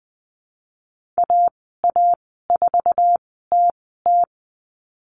Morse code, heard as a keyed tone, about 20 wpm, spaced as for 14 wpm, 700 Hz. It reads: AA4TT
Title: 082 Lesson 14g - Number 2 - Call Signs 20wpm